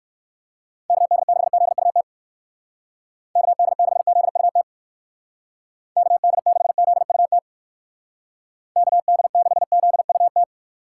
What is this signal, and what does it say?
Morse code, heard as a keyed tone, about 56 wpm, 700 Hz. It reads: KD6ZUT KD6ZUT KD6ZUT KD6ZUT
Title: Speed Racing - USA 2x3 Call Signs 45wpm